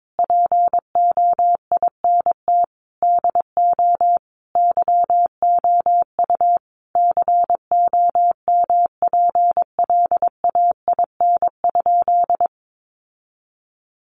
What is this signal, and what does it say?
Morse code, heard as a keyed tone, about 22 wpm, 700 Hz. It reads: POINT DO YOU COMPLAIN?
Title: Sentences from Top 1300 Words 22wpm